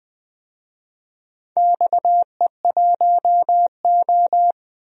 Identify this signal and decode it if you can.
Morse code, heard as a keyed tone, about 21 wpm, 700 Hz. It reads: XE1O
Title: DXCC-100 Call Signs - 4 Characters - 20wpm